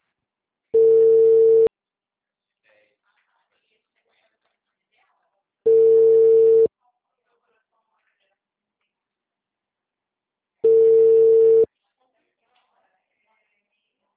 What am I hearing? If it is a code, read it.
Ringback tone (China, 450 Hz).
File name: out- -2003